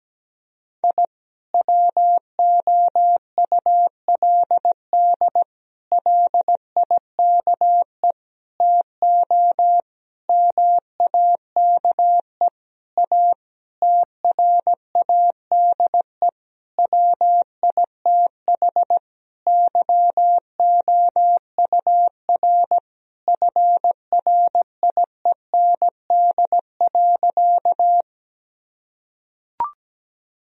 Morse code, heard as a keyed tone, about 17 wpm, 700 Hz. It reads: I WOULD LIKE TO MAKE A TRADE WITH YOUR FRIEND.
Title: Sentences from Top 1000 Words - Review Part 4 - 17wpm